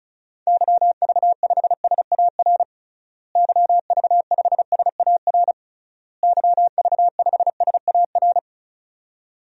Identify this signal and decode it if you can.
Morse code, heard as a keyed tone, about 36 wpm, 700 Hz. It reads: YV5SAR YV5SAR YV5SAR
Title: DXCC-100 Call Signs - 6 Characters - Repeated 3x - 35wpm